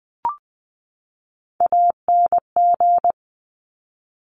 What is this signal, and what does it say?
Morse code, heard as a keyed tone, about 20 wpm, 700 Hz. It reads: ANG
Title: Top Three Letter Combinations in Words 20wpm